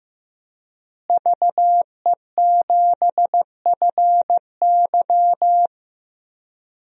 Morse code, heard as a keyed tone, about 15 wpm, 700 Hz. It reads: VE7FY
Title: DXCC-100 Call Signs - 5 Characters - Repeated 3x - 15wpm